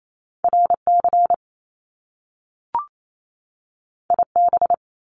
Morse code, heard as a keyed tone, about 28 wpm, 700 Hz. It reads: RC IB
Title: Top Two Letter Combinations in Words 28wpm